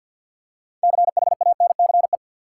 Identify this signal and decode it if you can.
Morse code, heard as a keyed tone, about 47 wpm, 700 Hz. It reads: CHANCE
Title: Six Letter Words - 50wpm